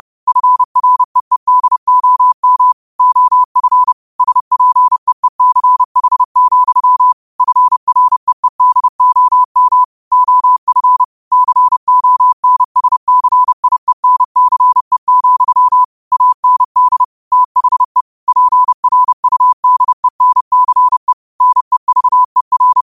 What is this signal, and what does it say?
Morse code, heard as a keyed tone, about 30 wpm, 1000 Hz. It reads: FREEDOM OF SPEECH, FREEDOM OF CONSCIENCE, AND THE PRUDENCE NEVER
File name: qotdfiles / 30wpm